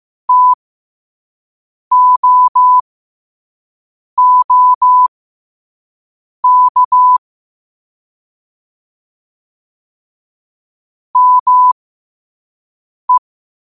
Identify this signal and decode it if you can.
Morse code, heard as a keyed tone, about 15 wpm, spaced as for 5 wpm, 1000 Hz. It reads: TOOK ME
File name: qotdfiles / 5wpm